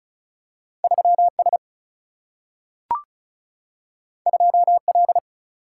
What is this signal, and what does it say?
Morse code, heard as a keyed tone, about 36 wpm, 700 Hz. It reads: 3 S 2 L
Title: Single-Letter Single-Number 35wpm